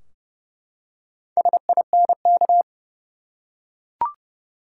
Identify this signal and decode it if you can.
Morse code, heard as a keyed tone, about 30 wpm, 700 Hz. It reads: SINK
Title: Four Letter Words - 30wpm